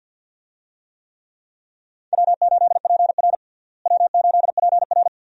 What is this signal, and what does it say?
Morse code, heard as a keyed tone, about 48 wpm, 700 Hz. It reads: W8PR W8PR
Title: USA Call Signs - 4 Characters - Repeated 2x - 50wpm